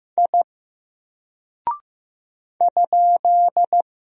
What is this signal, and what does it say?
Morse code, heard as a keyed tone, about 15 wpm, 700 Hz. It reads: I ?
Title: Single Character (Letter:Number:Punctuation:Prosign) 15wpm